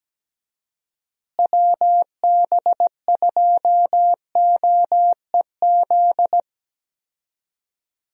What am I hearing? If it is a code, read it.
Morse code, heard as a keyed tone, about 17 wpm, 700 Hz. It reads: WB2OEZ
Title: USA Call Signs - 6 Characters - Repeated 2x - 17wpm